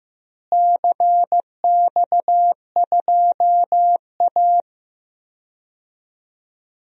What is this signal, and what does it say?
Morse code, heard as a keyed tone, about 15 wpm, 700 Hz. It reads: CX2A
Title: DXCC-100 Call Signs - 4 Characters - Repeated 3x - 15wpm